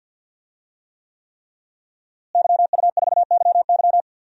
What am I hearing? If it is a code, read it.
Morse code, heard as a keyed tone, about 53 wpm, 700 Hz. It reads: YU4YY
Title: DXCC-100 Call Signs - 5 Characters - Repeated 3x - 50wpm